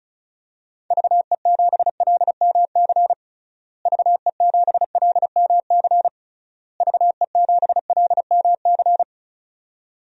Morse code, heard as a keyed tone, about 34 wpm, 700 Hz. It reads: VE7LMC VE7LMC VE7LMC
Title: DXCC-100 Call Signs - 6 Characters - Repeated 3x - 35wpm